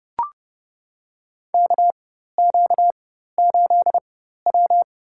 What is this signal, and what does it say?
Morse code, heard as a keyed tone, about 32 wpm, spaced as for 14 wpm, 700 Hz. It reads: KQ8W
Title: USA Call Signs - 4 Characters - 4x Character Spacing - 30wpm